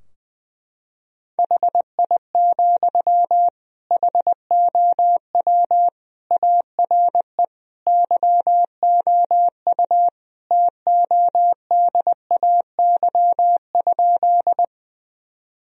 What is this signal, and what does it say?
Morse code, heard as a keyed tone, about 21 wpm, 700 Hz. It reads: HI, HOW ARE YOU TODAY?
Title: Sentences from Top 1000 Words - Review Part 3 - 20wpm